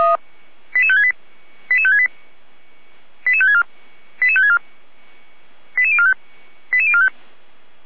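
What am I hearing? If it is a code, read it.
Five-tone selective call (ZVEI), heard as quick runs of five tones, tones of about 70 ms. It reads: 88568 88568 88564 88564 88045 88045
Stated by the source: ZVEI five-tone call received by SDR.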